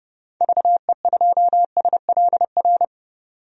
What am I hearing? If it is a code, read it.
Morse code, heard as a keyed tone, about 29 wpm, 690 Hz. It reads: VE2SLR
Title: DXCC-100 Call Signs - 6 Characters - 30wpm